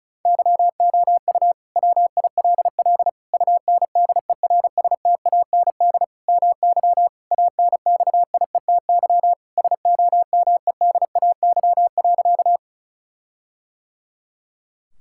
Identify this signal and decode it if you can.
Morse code, heard as a keyed tone, about 34 wpm, 700 Hz. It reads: YOU WILL UNDERSTAND MY ANXIETY SOMEDAY.
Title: Sentences from Top 1600 Words 35wpm